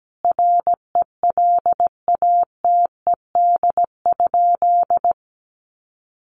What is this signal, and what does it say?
Morse code, heard as a keyed tone, about 17 wpm, 700 Hz. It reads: RELATED?
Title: Sentences from Top 1400 Words 17wpm